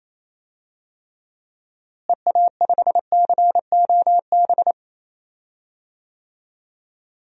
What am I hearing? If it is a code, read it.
Morse code, heard as a keyed tone, about 28 wpm, 700 Hz. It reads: EA5COB
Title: DXCC-100 Call Signs - 6 Characters - Repeated 3x - 28wpm